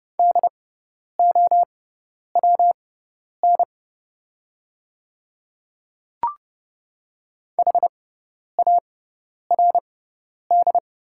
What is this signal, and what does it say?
Morse code, heard as a keyed tone, about 30 wpm, spaced as for 10 wpm, 700 Hz. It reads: DOWN HARD
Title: 6x Character Spacing - Top 300 Words - 30wpm